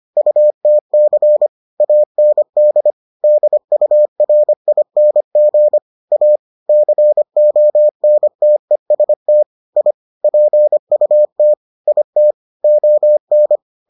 Morse code, heard as a keyed tone, about 25 wpm, 600 Hz. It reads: UTC AND DURING A CONTEST I PUT IT ON